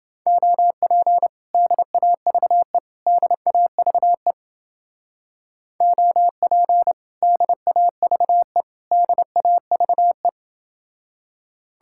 Morse code, heard as a keyed tone, about 31 wpm, 700 Hz. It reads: OP DAVE DAVE OP DAVE DAVE